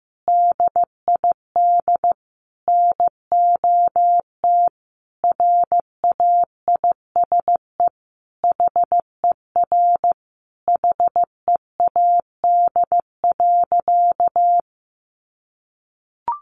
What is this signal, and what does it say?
Morse code, heard as a keyed tone, about 15 wpm, 700 Hz. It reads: DID NOT RAISE HER HEAD.
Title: Sentences from Top 1400 Words 15wpm